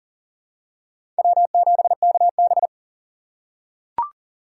Morse code, heard as a keyed tone, about 40 wpm, 700 Hz. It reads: W7KB
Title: USA Call Signs - 4 Characters - 2x Character Spacing - 40wpm